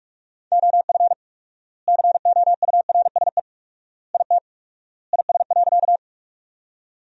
Morse code, heard as a keyed tone, about 47 wpm, 690 Hz. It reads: OF COURSE IT IS.
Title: 4x Word-Spacing - Sentences from Top 500 Words - 45wpm